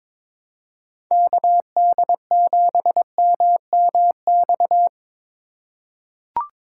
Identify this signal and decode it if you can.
Morse code, heard as a keyed tone, about 22 wpm, 700 Hz. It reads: KD7MMX
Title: USA Call Signs - 6 Characters - 4x Character Spacing - 22wpm